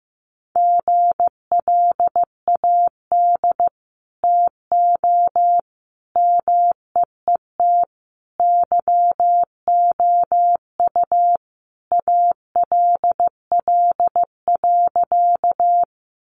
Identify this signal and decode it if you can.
Morse code, heard as a keyed tone, about 15 wpm, 700 Hz. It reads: GLAD TO MEET YOU ALL.
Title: Sentences from Top 1100 Words 15wpm